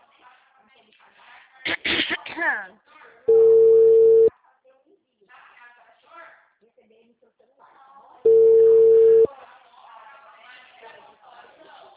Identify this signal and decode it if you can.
Ringback tone (China, 450 Hz), with speech.